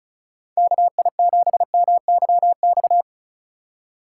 Morse code, heard as a keyed tone, about 34 wpm, 700 Hz. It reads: KI7MYX
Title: USA Call Signs - 6 Characters - 3x Character Spacing - 35wpm